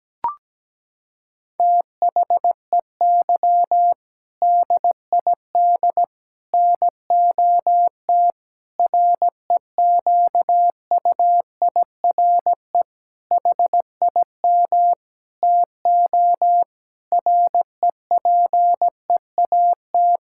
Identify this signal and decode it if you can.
Morse code, heard as a keyed tone, about 17 wpm, 700 Hz. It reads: THEY DID NOT REQUIRE HIM TO REPEAT
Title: Sentences from Top 1600 Words 17wpm